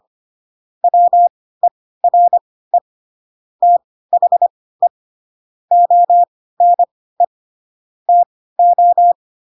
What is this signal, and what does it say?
Morse code, heard as a keyed tone, about 25 wpm, spaced as for 16 wpm, 700 Hz. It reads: WERE THE ONE TO
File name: Pinocchio / Pinocchio ch2 10 25 ik1hge